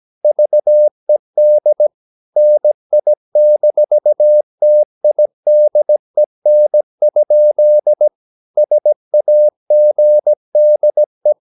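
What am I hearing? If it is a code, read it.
Morse code, heard as a keyed tone, about 17 wpm, 600 Hz. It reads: VED NI-TIDEN? SAGDE
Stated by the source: Slow (QRS) Morse practice at 17 wpm, from CW med Gnister.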